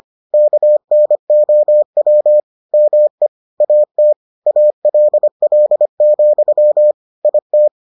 Morse code, heard as a keyed tone, about 25 wpm, 600 Hz. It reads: KNOW ME AT ALL, IT